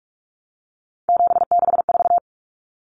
Morse code, heard as a keyed tone, about 44 wpm, 700 Hz. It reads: 764
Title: Numbers - 3-Digit 45wpm